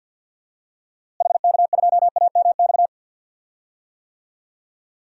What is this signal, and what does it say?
Morse code, heard as a keyed tone, about 52 wpm, 700 Hz. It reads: HK2AMX
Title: DXCC-100 Call Signs - 6 Characters - Repeated 2x - 50wpm